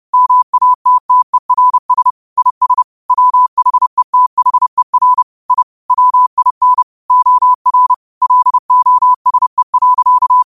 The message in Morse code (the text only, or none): MATTERS IS WHETHER I WIN OR LOSE.